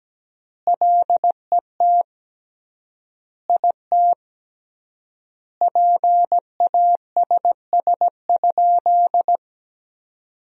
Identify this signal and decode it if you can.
Morse code, heard as a keyed tone, about 17 wpm, 700 Hz. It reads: LET IT PASS?
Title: 3x Word-Spacing - Sentences from Top 400 Words - 17wpm